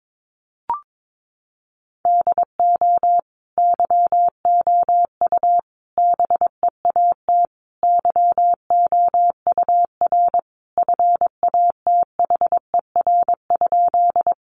Morse code, heard as a keyed tone, about 22 wpm, 700 Hz. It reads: DO YOU BEAT YOUR FATHER?
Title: Sentences from Top 1300 Words 22wpm